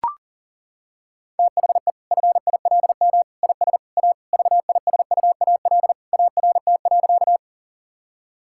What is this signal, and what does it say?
Morse code, heard as a keyed tone, about 40 wpm, 700 Hz. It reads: THE FILM IS A VISUAL ART.
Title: Sentences from Top 1000 Words - Review Part 6 - 40wpm